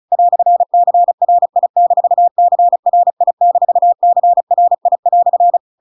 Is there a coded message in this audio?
Yes: Morse code from 0.1 to 5.6 s, heard as a keyed tone, about 34 wpm, 700 Hz. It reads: "CRI-CRI-CRI"